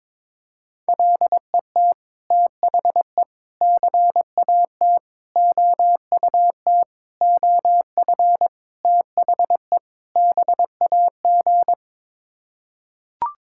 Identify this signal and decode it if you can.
Morse code, heard as a keyed tone, about 22 wpm, 700 Hz. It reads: LET THE CAT OUT OF THE BAG
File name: practice / 22wpm / Sets of 7 Words - 22wpm